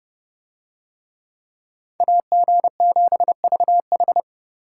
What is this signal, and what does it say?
Morse code, heard as a keyed tone, about 31 wpm, 700 Hz. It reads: AG7VH